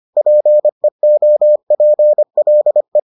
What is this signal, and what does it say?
Morse code, heard as a keyed tone, about 25 wpm, 600 Hz. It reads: PEOPLE